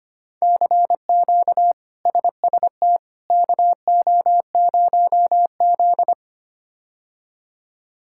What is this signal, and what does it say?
Morse code, heard as a keyed tone, about 25 wpm, 700 Hz. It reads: CQ SST KO0Z
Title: K1USN SST QSO Practice - 25wpm